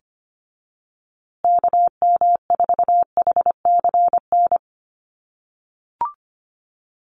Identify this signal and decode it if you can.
Morse code, heard as a keyed tone, about 25 wpm, 700 Hz. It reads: KM4HCN
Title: USA Call Signs Repeated 2x 25wpm